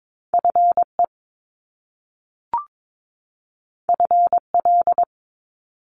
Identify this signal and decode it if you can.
Morse code, heard as a keyed tone, about 22 wpm, 700 Hz. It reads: FE FL
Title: Top Two Letter Combinations in Words 22wpm